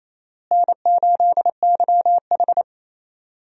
Morse code, heard as a keyed tone, about 28 wpm, 700 Hz. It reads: N8YH
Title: USA Call Signs - 4 Characters - Repeated 2x - 28wpm